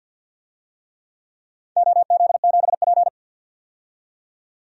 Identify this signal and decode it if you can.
Morse code, heard as a keyed tone, about 52 wpm, 700 Hz. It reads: OZ7P
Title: DXCC-100 Call Signs - 4 Characters - 50wpm